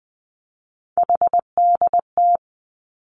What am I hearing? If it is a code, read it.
Morse code, heard as a keyed tone, about 20 wpm, 700 Hz. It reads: HDT
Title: Three Challenging Letter-Number Combinations 20wpm